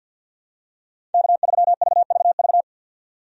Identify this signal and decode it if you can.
Morse code, heard as a keyed tone, about 52 wpm, 700 Hz. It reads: K3VVV
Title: USA Call Signs - 5 Characters - 50wpm